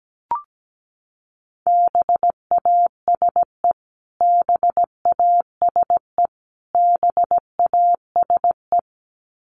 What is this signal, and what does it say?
Morse code, heard as a keyed tone, about 17 wpm, 700 Hz. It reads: BASE BASE BASE
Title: Top 300 Words Repeated 17wpm